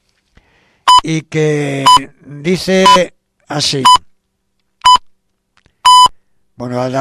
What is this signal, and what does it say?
Radio time signal (time pips), with speech. It is heard from 0.9 to 6.1 s.